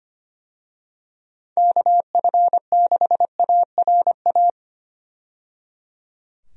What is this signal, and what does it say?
Morse code, heard as a keyed tone, about 25 wpm, 700 Hz. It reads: KF6ARA
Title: USA Call Signs - 6 Characters - 25wpm